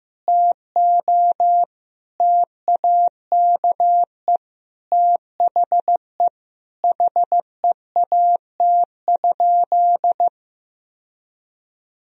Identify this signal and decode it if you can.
Morse code, heard as a keyed tone, about 15 wpm, 700 Hz. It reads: TO TAKE THE HEAT?
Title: Sentences from Top 1000 Words - Review Part 3 - 15wpm